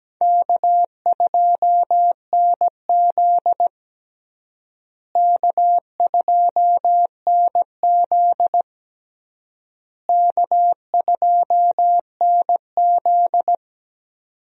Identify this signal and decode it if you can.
Morse code, heard as a keyed tone, about 17 wpm, 700 Hz. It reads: K2NZ K2NZ K2NZ
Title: USA Call Signs - 4 Characters - Repeated 3x - 17wpm